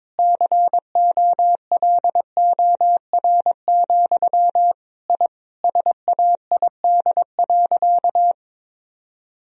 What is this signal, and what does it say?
Morse code, heard as a keyed tone, about 22 wpm, 700 Hz. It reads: COLOR, I SAID.
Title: Sentences from Top 1000 Words - Review Part 2 - 22wpm